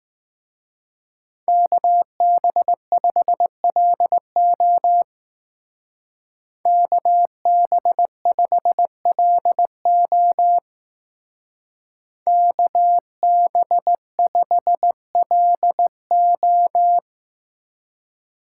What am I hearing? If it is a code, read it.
Morse code, heard as a keyed tone, about 19 wpm, 700 Hz. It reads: KB5LO KB5LO KB5LO
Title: Speed Racing - USA Call Signs 15wpm